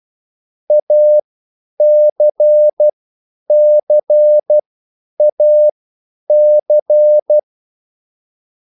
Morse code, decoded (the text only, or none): ACCAC